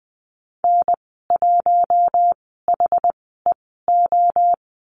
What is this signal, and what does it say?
Morse code, heard as a keyed tone, about 20 wpm, spaced as for 14 wpm, 700 Hz. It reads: N1HEO
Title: 122 Lesson 20g - Number 3 - Call Signs 20wpm